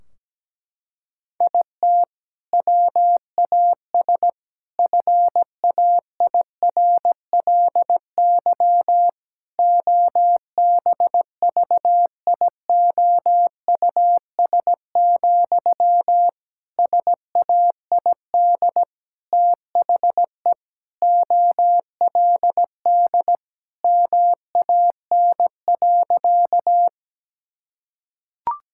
Morse code, heard as a keyed tone, about 17 wpm, 700 Hz. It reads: IT WAS FAIRLY OBVIOUS, SAID THE OLD MAN.
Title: Sentences from Top 1600 Words 17wpm